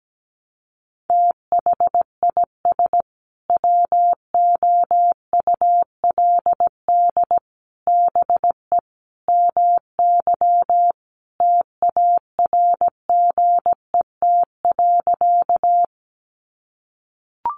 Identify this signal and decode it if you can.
Morse code, heard as a keyed tone, about 17 wpm, 700 Hz. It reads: THIS WOULD BE MY TARGET.
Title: Sentences from Top 1200 Words 17wpm